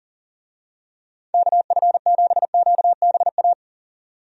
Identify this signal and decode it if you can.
Morse code, heard as a keyed tone, about 39 wpm, 700 Hz. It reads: KF7QBA
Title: USA Call Signs - 6 Characters - 40wpm